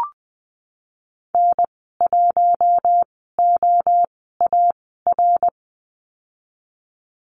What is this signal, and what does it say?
Morse code, heard as a keyed tone, about 20 wpm, spaced as for 15 wpm, 700 Hz. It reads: N1OAR
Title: 055 Lesson 10g - Letter r - Call Signs 20wpm